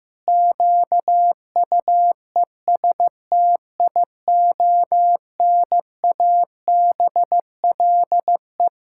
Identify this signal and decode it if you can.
Morse code, heard as a keyed tone, about 15 wpm, 700 Hz. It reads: QUESTIONABLE
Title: Pangram Challenge v2 - 15wpm